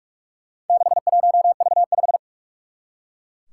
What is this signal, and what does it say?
Morse code, heard as a keyed tone, about 44 wpm, 700 Hz. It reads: 61V5
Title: Four Challenging Letter-Number Combinations 45wpm